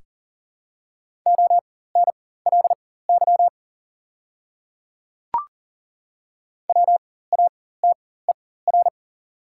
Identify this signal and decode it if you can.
Morse code, heard as a keyed tone, about 40 wpm, spaced as for 19 wpm, 700 Hz. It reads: ONLY WATER
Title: 4x Character Spacing - Top 200 Words - 40wpm